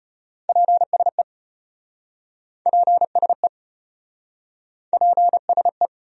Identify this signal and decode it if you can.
Morse code, heard as a keyed tone, about 33 wpm, 700 Hz. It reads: PSE PSE PSE